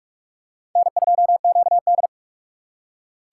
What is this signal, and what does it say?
Morse code, heard as a keyed tone, about 46 wpm, 700 Hz. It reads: N2QD
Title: USA Call Signs - 4 Characters - 4x Character Spacing - 45wpm